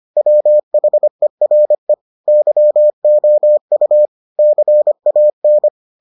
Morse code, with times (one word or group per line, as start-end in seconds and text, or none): 0.2-1.9 WHERE
2.3-4.1 YOU
4.4-5.7 CAN